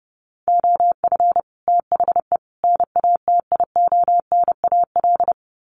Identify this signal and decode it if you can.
Morse code, heard as a keyed tone, about 30 wpm, 700 Hz. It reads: OF THE NATIONAL
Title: Sets of 3 Words 30wpm